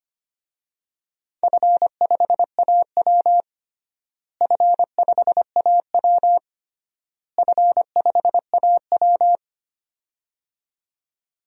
Morse code, heard as a keyed tone, about 25 wpm, 700 Hz. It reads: F5AW F5AW F5AW
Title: DXCC-100 Call Signs - 4 Characters - Repeated 3x - 25wpm